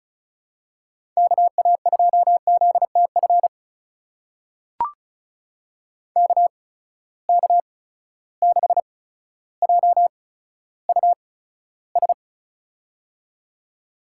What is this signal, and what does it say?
Morse code, heard as a keyed tone, about 36 wpm, spaced as for 15 wpm, 700 Hz. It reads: KA2ZTF KK6JUS
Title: USA Call Signs - 6 Characters - 8x Character Spacing - 35wpm